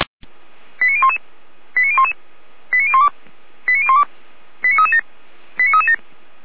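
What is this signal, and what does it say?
Five-tone selective call (ZVEI), heard as quick runs of five tones, tones of about 70 ms. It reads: 89011 89011 89012 89012 89337 89337
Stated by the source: ZVEI five-tone call 89067 received by SDR, from fire brigade Feuerwehr Orlamünde's audio archive (f85360).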